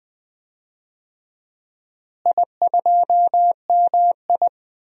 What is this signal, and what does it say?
Morse code, heard as a keyed tone, about 21 wpm, 700 Hz. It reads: I2MI
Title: DXCC-100 Call Signs - 4 Characters - Repeated 2x - 20wpm